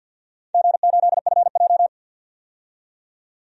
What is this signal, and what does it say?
Morse code, heard as a keyed tone, about 50 wpm, 700 Hz. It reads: G8FJ
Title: DXCC-100 Call Signs - 4 Characters - 50wpm